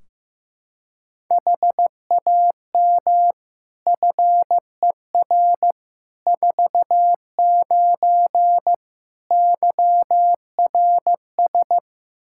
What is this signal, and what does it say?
Morse code, heard as a keyed tone, about 15 wpm, 700 Hz. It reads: HAM FER 49 YRS
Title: Practice QSO - Exchange 3 - More Info - 15wpm